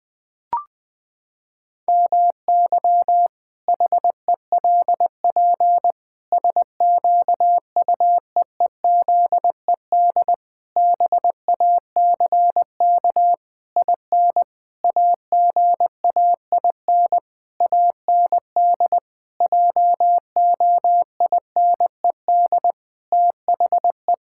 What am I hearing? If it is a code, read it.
Morse code, heard as a keyed tone, about 20 wpm, 700 Hz. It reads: MY HELP SQUEEZED BACK IN AGAIN AND JOINED THE
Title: Pangram Challenge v2 - 20wpm